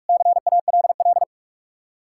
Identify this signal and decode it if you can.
Morse code, heard as a keyed tone, about 66 wpm, spaced as for 55 wpm, 700 Hz. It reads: CALL
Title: Speed-Racing Common QSO Phrases Encore - 45wpm